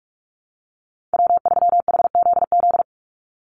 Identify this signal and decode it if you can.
Morse code, heard as a keyed tone, about 44 wpm, 700 Hz. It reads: W3HZZ
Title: Call Signs Repeated 3x - 45wpm